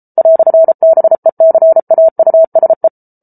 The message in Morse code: "BECAUSE

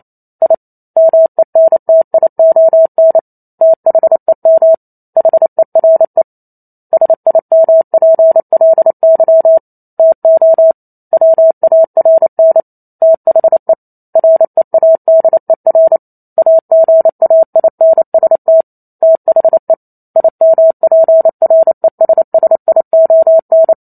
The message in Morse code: I MENTION THEM HERE SIMPLY TO WARN THE READER AGAINST THE IMPRESSION

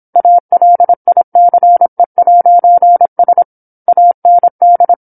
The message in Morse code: ALICE'S AND

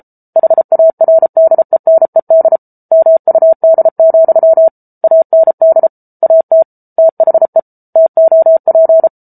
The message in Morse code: HARDENED MUD, AND AT THE TOP

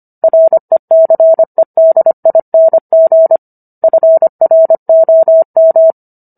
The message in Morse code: RECEDING FROM

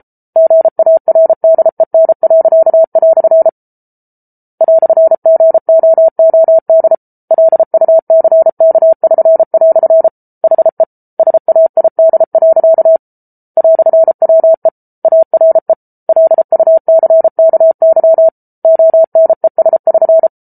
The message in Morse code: GARDEN." "GOOD LUCK<SN>" HE SAID. "WE ARE LUCKY ONES<SN>

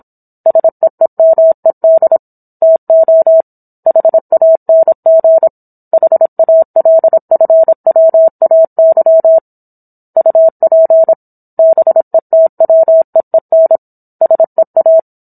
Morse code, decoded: SEEMED TO HANG HALFWAY UP BETWEEN SEA

